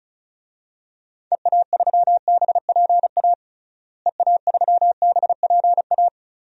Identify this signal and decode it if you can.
Morse code, heard as a keyed tone, about 34 wpm, 700 Hz. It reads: EA3BPA EA3BPA
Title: DXCC-100 Call Signs - 6 Characters - Repeated 2x - 35wpm